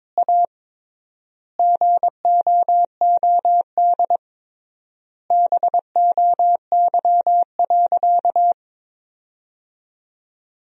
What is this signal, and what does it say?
Morse code, heard as a keyed tone, about 21 wpm, 700 Hz. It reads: A GOOD BOY.
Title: 3x Word-Spacing - Sentences from Top 200 Words - 22wpm